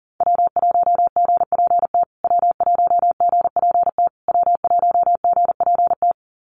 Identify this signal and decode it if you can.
Morse code, heard as a keyed tone, about 40 wpm, 700 Hz. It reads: W1GPT W1GPT W1GPT
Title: USA Call Signs Repeated 3x 40wpm